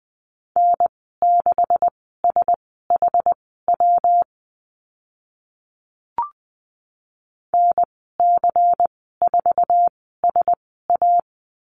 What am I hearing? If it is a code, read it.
Morse code, heard as a keyed tone, about 20 wpm, spaced as for 14 wpm, 700 Hz. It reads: N6SHW NC4SA
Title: 128 Lesson 21g - Number 6 - Call Signs 20wpm